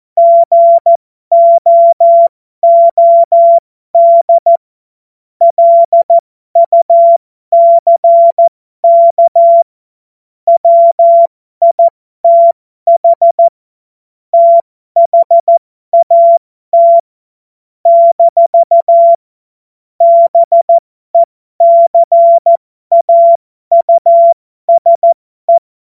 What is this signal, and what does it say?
Morse code, heard as a keyed tone, about 14 wpm, spaced as for 12 wpm, 680 Hz. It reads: GOOD LUCK WITH THAT - BECAUSE
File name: Morsekurs / Witze